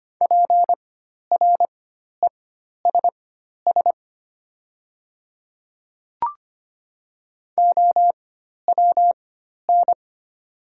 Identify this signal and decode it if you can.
Morse code, heard as a keyed tone, about 25 wpm, spaced as for 12 wpm, 700 Hz. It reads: PRESS OWN
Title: 4x Character Spacing - Top 300 Words - 25wpm